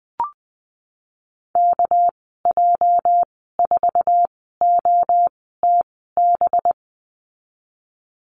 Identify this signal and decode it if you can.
Morse code, heard as a keyed tone, about 20 wpm, spaced as for 15 wpm, 700 Hz. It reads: KJ4OTB
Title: 213 Lesson 34g - Number 8 - Call Signs 20wpm